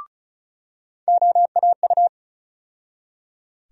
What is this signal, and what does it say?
Morse code, heard as a keyed tone, about 34 wpm, 700 Hz. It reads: OAU
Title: USA Call Sign Postfixes - 3 Letters 35wpm